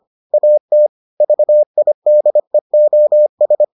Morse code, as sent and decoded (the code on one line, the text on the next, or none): .- - / ...- .. -.. --- ...
AT VIDEOS